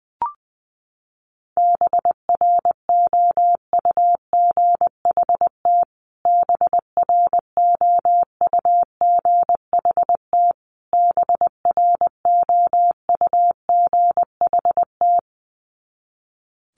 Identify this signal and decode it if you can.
Morse code, heard as a keyed tone, about 20 wpm, 700 Hz. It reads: BROUGHT BROUGHT BROUGHT
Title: Top 500 Words Repeated 20wpm